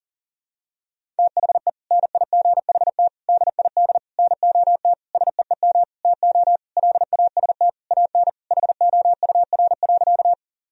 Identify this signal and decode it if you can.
Morse code, heard as a keyed tone, about 40 wpm, 700 Hz. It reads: THE NIGHT DID NOT SEEM TO LAST AN HOUR.
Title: Sentences from Top 1000 Words - Review Part 1 - 40wpm